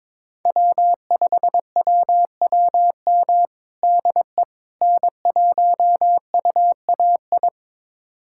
Morse code, heard as a keyed tone, about 22 wpm, 700 Hz. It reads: W5WWM DE N1UAI
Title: USA Callsign DE Callsign - 5 Characters - 22wpm